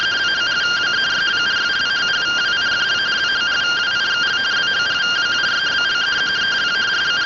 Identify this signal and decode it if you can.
Two-tone teleprinter signal, about 170 Hz shift.